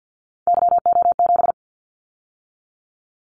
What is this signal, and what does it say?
Morse code, heard as a keyed tone, about 50 wpm, 700 Hz. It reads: YO7
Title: Callsign Prefixes for Top 25 DXCC - 50wpm